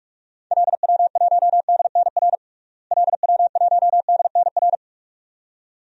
Morse code, heard as a keyed tone, about 41 wpm, 700 Hz. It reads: LW1DNR LW1DNR
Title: DXCC-100 Call Signs - 6 Characters - Repeated 2x - 45wpm